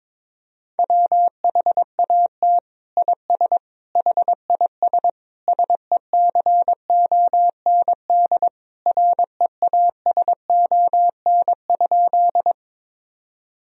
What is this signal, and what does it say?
Morse code, heard as a keyed tone, about 22 wpm, 700 Hz. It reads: WHAT IS HIS SECOND REASON?
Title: Sentences from Top 1000 Words - Review Part 1 - 22wpm